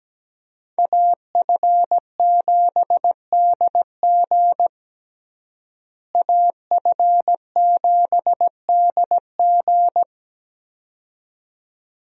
Morse code, heard as a keyed tone, about 17 wpm, 700 Hz. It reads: AF7DG AF7DG
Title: USA Call Signs - 5 Characters - Repeated 2x - 17wpm